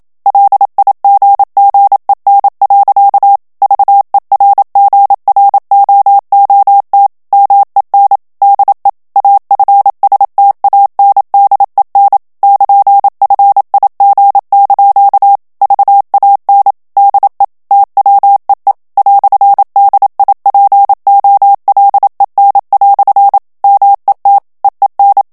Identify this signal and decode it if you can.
Morse code, heard as a keyed tone, about 27 wpm, 800 Hz. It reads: LIGGEN. VERGROOT MEN DE AFSTANDEN <KN>FIG) VAN DE TWEE "DIPOLEN" MET EEN
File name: HOLLANDAIS texte clair 1 vitesse 26 mots minute